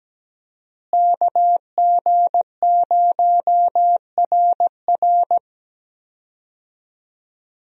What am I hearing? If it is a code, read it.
Morse code, heard as a keyed tone, about 17 wpm, 700 Hz. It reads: KG0RR
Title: USA Call Signs - 5 Characters - Repeated 3x - 17wpm